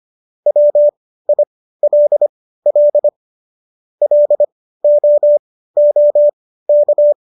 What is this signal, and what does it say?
Morse code, heard as a keyed tone, about 25 wpm, spaced as for 15 wpm, 590 Hz. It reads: WILL LOOK